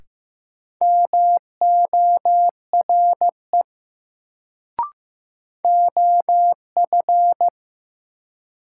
Morse code, heard as a keyed tone, about 15 wpm, 700 Hz. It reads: MORE OF
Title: Sentences from Top 100 Words - New Word Drill - 15wpm